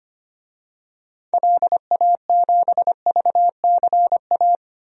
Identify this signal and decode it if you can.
Morse code, heard as a keyed tone, about 25 wpm, 700 Hz. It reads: LA7VCA
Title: DXCC-100 Call Signs - 6 Characters - 25wpm